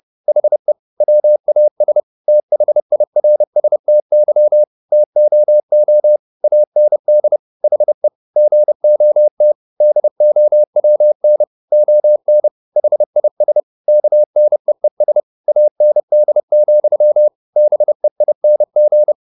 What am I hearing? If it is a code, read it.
Morse code, heard as a keyed tone, about 30 wpm, 600 Hz. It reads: HE WAS THIRSTY TOO AND HE GOT DOWN ON HIS KNEES AND, BEING
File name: the old man and the sea / the old man and the sea 1 30wpm